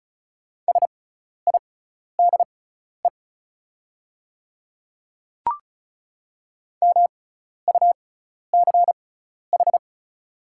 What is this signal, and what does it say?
Morse code, heard as a keyed tone, about 36 wpm, spaced as for 12 wpm, 690 Hz. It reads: SIDE MUCH
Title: 6x Character Spacing - Top 300 Words - 35wpm